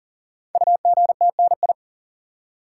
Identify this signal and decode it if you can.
Morse code, heard as a keyed tone, about 42 wpm, 690 Hz. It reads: UGTNI
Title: Code Groups - 5 Letters - 40wpm